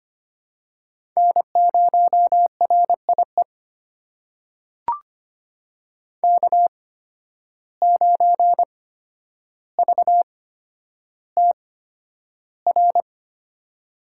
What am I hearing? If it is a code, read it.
Morse code, heard as a keyed tone, about 25 wpm, spaced as for 11 wpm, 700 Hz. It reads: N0RIE K9VTR
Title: USA Call Signs - 5 Characters - 8x Character Spacing - 25wpm